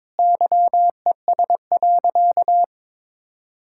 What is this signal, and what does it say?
Morse code, heard as a keyed tone, about 22 wpm, 700 Hz. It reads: YES.
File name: practice / 22wpm / Sentences with 4-Word Phrases - 22wpm